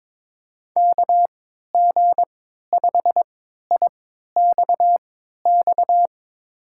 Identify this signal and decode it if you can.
Morse code, heard as a keyed tone, about 22 wpm, spaced as for 12 wpm, 700 Hz. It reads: KG5IXX
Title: USA Call Signs - 6 Characters - 3x Character Spacing - 22wpm